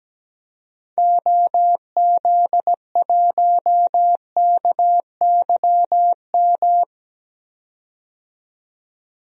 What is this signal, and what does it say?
Morse code, heard as a keyed tone, about 17 wpm, 700 Hz. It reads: OZ1KYM
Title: DXCC-100 Call Signs - 6 Characters - 17wpm